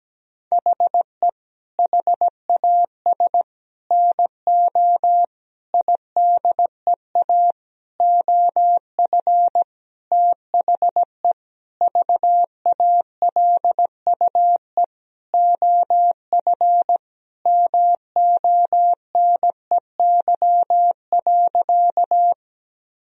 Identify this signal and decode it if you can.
Morse code, heard as a keyed tone, about 17 wpm, 700 Hz. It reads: HE HAS NO IDEA OF THE VALUE OF MONEY.
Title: Sentences from Top 1000 Words - Review Part 3 - 17wpm